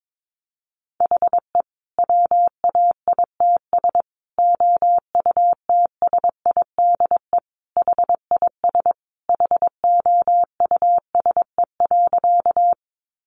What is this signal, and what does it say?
Morse code, heard as a keyed tone, about 22 wpm, 700 Hz. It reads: HE WAITS OUTSIDE HIS HOUSE.
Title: Additional Easy Phrases 22wpm